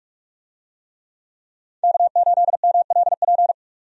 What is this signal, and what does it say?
Morse code, heard as a keyed tone, about 41 wpm, 700 Hz. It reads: K8MLP
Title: USA Call Signs - 5 Characters - 45wpm